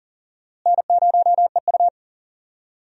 Morse code, heard as a keyed tone, about 40 wpm, 700 Hz. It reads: N0EU
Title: USA Call Signs - 4 Characters - Repeated 3x - 40wpm